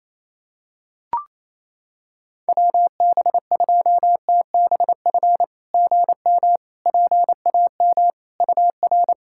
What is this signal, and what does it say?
Morse code, heard as a keyed tone, about 28 wpm, 700 Hz. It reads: WB2TBF GM PAM UR
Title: SOTA QSO Practice 28wpm